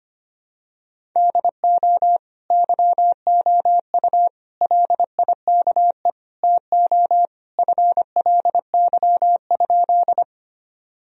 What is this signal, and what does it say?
Morse code, heard as a keyed tone, about 25 wpm, 700 Hz. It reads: DO YOU LIKE TO FLY?
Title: Sentences from Top 1000 Words - Review Part 2 - 25wpm